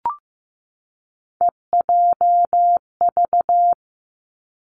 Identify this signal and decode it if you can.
Morse code, heard as a keyed tone, about 15 wpm, 700 Hz. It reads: EJV
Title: Three Challenging Letter-Number Combinations 15wpm